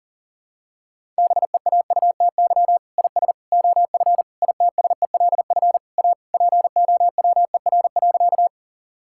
Morse code, heard as a keyed tone, about 37 wpm, 700 Hz. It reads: BEAUTY IS OF ITSELF A POWER.
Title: Sentences from Top 1000 Words - Review Part 5 - 40wpm